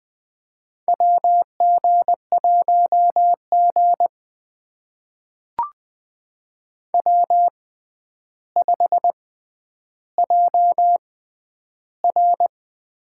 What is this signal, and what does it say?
Morse code, heard as a keyed tone, about 20 wpm, spaced as for 10 wpm, 700 Hz. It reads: WG1G W5JR
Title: USA Call Signs - 4 Characters - 6x Character Spacing - 20wpm